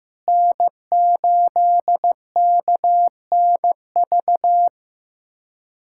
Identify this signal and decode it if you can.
Morse code, heard as a keyed tone, about 15 wpm, 700 Hz. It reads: N8KNV
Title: USA Callsign DE Callsign - 5 Characters - 15wpm